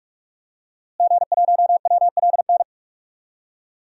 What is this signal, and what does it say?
Morse code, heard as a keyed tone, about 46 wpm, 700 Hz. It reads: G1WLN
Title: DXCC-100 Call Signs - 5 Characters - Repeated 3x - 45wpm